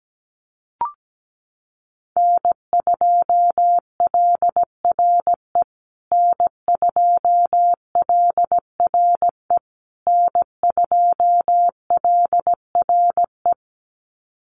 Morse code, heard as a keyed tone, about 17 wpm, 700 Hz. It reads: N2LRE N2LRE N2LRE
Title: USA Call Signs Repeated 3x 17wpm